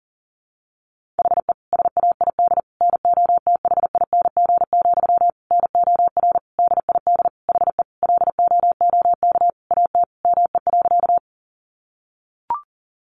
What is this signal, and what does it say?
Morse code, heard as a keyed tone, about 40 wpm, 700 Hz. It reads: HE SAID NOTHING, NOR DID HE LOOK AT ME.
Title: Sentences from Top 1500 Words 40wpm